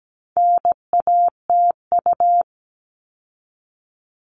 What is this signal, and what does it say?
Morse code, heard as a keyed tone, about 17 wpm, 700 Hz. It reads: NATU
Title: All Cut Numbers - 4-Digit 17wpm